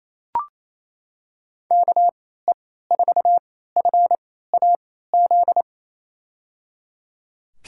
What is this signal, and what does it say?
Morse code, heard as a keyed tone, about 28 wpm, spaced as for 16 wpm, 700 Hz. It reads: KE4FAZ
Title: USA Call Signs - 6 Characters - 3x Character Spacing - 28wpm